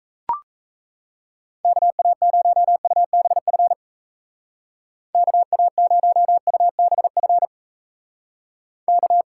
Morse code, heard as a keyed tone, about 40 wpm, 700 Hz. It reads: KA0UBF KA0UBF K